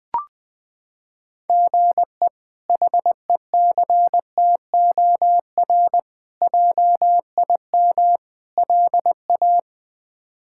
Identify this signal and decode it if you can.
Morse code, heard as a keyed tone, about 20 wpm, 700 Hz. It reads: GE HECTOR JIM LA
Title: K1USN SST QSO Practice - 20wpm